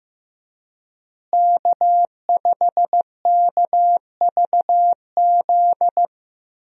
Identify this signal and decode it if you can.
Morse code, heard as a keyed tone, about 15 wpm, 700 Hz. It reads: K5KVZ
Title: USA Call Signs - 5 Characters - Repeated 3x - 15wpm